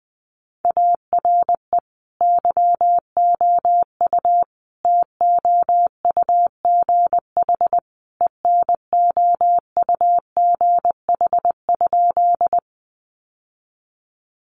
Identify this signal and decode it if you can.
Morse code, heard as a keyed tone, about 19 wpm, 700 Hz. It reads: ARE YOU TOUGH ENOUGH?
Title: Sentences from Top 1200 Words 20wpm